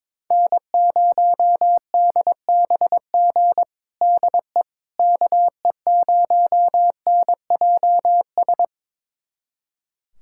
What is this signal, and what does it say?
Morse code, heard as a keyed tone, about 22 wpm, 700 Hz. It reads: N0DBG DE KE0NJS
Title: USA Callsign DE Callsign 22wpm